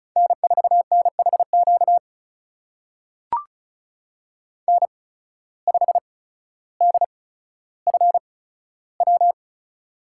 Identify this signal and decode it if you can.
Morse code, heard as a keyed tone, about 36 wpm, spaced as for 15 wpm, 700 Hz. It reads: N4NHQ N5DFW
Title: USA Call Signs - 8x Character Spacing - 35wpm